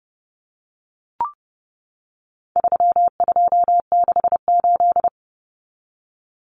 Morse code, heard as a keyed tone, about 30 wpm, 700 Hz. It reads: 3268
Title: Numbers - 4-Digit 30wpm